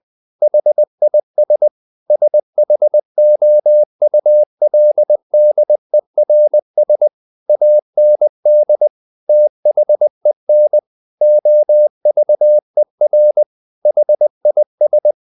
Morse code, heard as a keyed tone, about 20 wpm, 600 Hz. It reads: HIS SHOULDERS AND THEN OVER HIS
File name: the old man and the sea part 2 20wpm